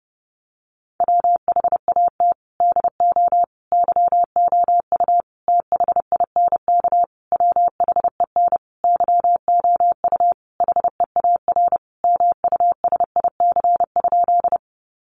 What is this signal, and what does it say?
Morse code, heard as a keyed tone, about 30 wpm, 700 Hz. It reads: WHAT DO YOU THINK WHEN YOU HEAR MUSIC?
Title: Sentences from Top 300 Words 30wpm